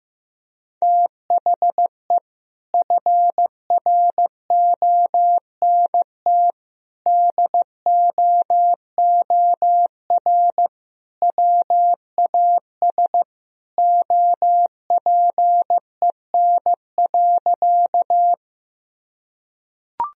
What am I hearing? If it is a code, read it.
Morse code, heard as a keyed tone, about 16 wpm, 700 Hz. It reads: THE FRONT DOOR WAS OPEN.
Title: Sentences from Top 1000 Words - Review Part 2 - 15wpm